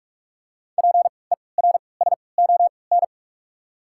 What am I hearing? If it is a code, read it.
Morse code, heard as a keyed tone, about 46 wpm, spaced as for 26 wpm, 700 Hz. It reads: PERSON
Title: 3x Character Spacing - Top 500 Words - 45wpm